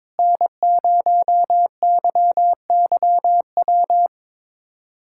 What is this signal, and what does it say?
Morse code, heard as a keyed tone, about 22 wpm, 700 Hz. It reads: N0YYW
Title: USA Callsign DE Callsign - 5 Characters - 22wpm